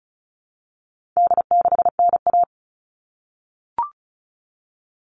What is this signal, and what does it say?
Morse code, heard as a keyed tone, about 34 wpm, 700 Hz. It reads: D6NA
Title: All Cut Numbers - 4-Digit 35wpm